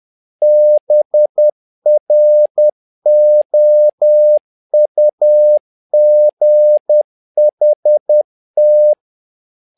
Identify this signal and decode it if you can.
Morse code, heard as a keyed tone, about 10 wpm, 600 Hz. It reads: BROUGHT
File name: learning center / tower trouble / tower trouble 10wpm